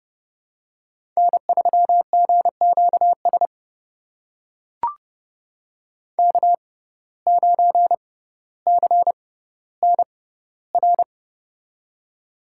Morse code, heard as a keyed tone, about 30 wpm, spaced as for 15 wpm, 700 Hz. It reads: N3GQS K9CNR
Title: USA Call Signs - 5 Characters - 6x Character Spacing - 30wpm